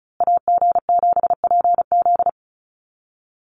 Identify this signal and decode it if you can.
Morse code, heard as a keyed tone, about 34 wpm, 700 Hz. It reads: AG7PZ
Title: USA Call Signs Repeated 2x 35wpm